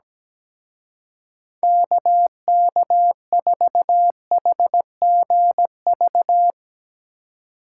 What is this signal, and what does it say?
Morse code, heard as a keyed tone, about 18 wpm, 700 Hz. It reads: KK4HGV
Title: Speed Racing - USA 2x3 Call Signs 17wpm